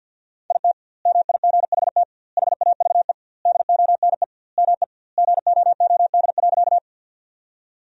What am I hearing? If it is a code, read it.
Morse code, heard as a keyed tone, about 50 wpm, 700 Hz. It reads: IT MIGHT HAVE DONE ME GOOD.
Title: 2x Word-Spacing - Sentences from Top 500 Words - 50wpm